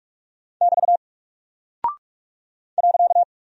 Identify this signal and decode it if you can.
Morse code, heard as a keyed tone, about 44 wpm, 700 Hz. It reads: <BT> .
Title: Single Character (Letter:Number:Punctuation:Prosign) Alt 45wpm